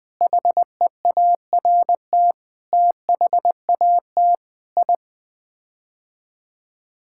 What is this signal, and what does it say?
Morse code, heard as a keyed tone, about 20 wpm, 700 Hz. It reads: HEART THAT I
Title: Sets of 8 Words - 20wpm